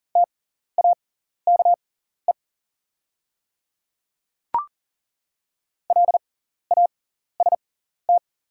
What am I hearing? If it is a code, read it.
Morse code, heard as a keyed tone, about 37 wpm, spaced as for 14 wpm, 690 Hz. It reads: TAKE LAST
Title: 6x Character Spacing - Top 500 Words - 40wpm